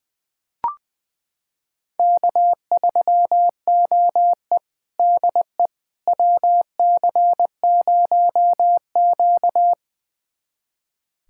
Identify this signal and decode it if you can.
Morse code, heard as a keyed tone, about 20 wpm, 700 Hz. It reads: K3OE DE WC0Q
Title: USA Callsign DE Callsign - 4 Characters - 20wpm